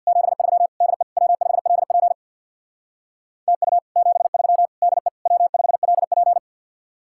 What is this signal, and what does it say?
Morse code, heard as a keyed tone, about 66 wpm, 700 Hz. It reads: 73 DE W5LP TU 73 DE W5LP